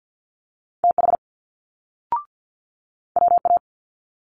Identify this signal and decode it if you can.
Morse code, heard as a keyed tone, about 50 wpm, 700 Hz. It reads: TH WA
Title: Multi-letter Phonemes Quick Fire 50wpm